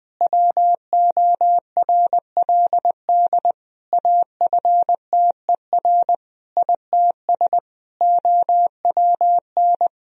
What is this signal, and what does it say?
Morse code, heard as a keyed tone, about 21 wpm, 700 Hz. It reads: WORLD AFTER ITS OWN
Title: Sentences from Top 1000 Words - Review Part 4 - 20wpm